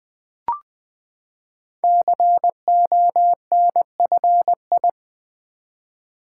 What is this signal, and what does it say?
Morse code, heard as a keyed tone, about 20 wpm, 700 Hz. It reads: CONFI
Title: Build A Word - 5 Letters At A Time - 12-Letter Words - 20wpm